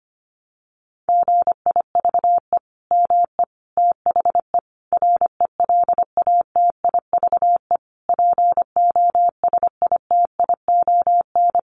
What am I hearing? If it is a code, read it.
Morse code, heard as a keyed tone, about 25 wpm, 700 Hz. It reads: GIVE ME THE RELATIVE POSITION
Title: Sentences from Top 1400 Words 25wpm